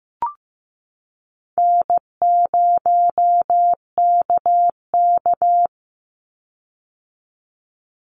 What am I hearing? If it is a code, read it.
Morse code, heard as a keyed tone, about 15 wpm, 700 Hz. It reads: N0KK
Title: Easy Call Signs - 15wpm